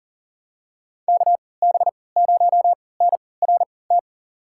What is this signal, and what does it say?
Morse code, heard as a keyed tone, about 37 wpm, spaced as for 22 wpm, 700 Hz. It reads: KB0NRT
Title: USA Call Signs - 6 Characters - 3x Character Spacing - 40wpm